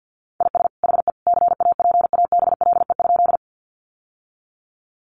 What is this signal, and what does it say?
Morse code, heard as a keyed tone, about 50 wpm, 700 Hz. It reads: IS HE CAPABLE?
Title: Sentences from Top 1200 Words 50wpm